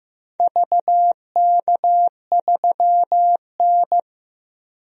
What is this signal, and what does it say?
Morse code, heard as a keyed tone, about 15 wpm, 700 Hz. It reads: VK3N
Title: DXCC-100 Call Signs - 4 Characters - 15wpm